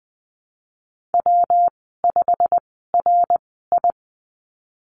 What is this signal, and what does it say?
Morse code, heard as a keyed tone, about 20 wpm, spaced as for 15 wpm, 700 Hz. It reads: W5RI
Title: 193 Lesson 31g - Letter v - Call Signs 20wpm